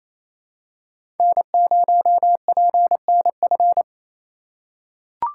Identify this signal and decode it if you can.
Morse code, heard as a keyed tone, about 28 wpm, 700 Hz. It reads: N0PNF
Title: USA Call Signs - 5 Characters - 2x Character Spacing - 28wpm